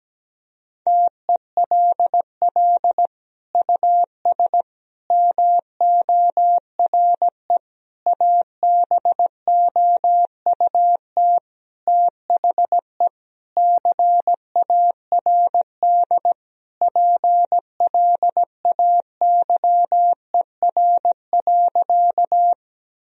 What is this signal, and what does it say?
Morse code, heard as a keyed tone, about 17 wpm, 700 Hz. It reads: TELL US MORE ABOUT THE CARD PLAYER.
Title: Sentences from Top 1000 Words - Review Part 4 - 17wpm